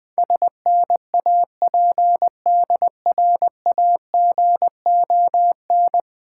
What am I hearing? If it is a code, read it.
Morse code, heard as a keyed tone, about 20 wpm, 690 Hz. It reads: SNAPDRAGON
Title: Difficult Words with Hints - 20wpm